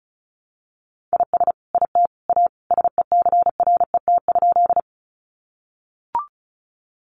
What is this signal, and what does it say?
Morse code, heard as a keyed tone, about 34 wpm, 700 Hz. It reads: IS IT A SECRET?
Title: Sentences from Top 1200 Words 35wpm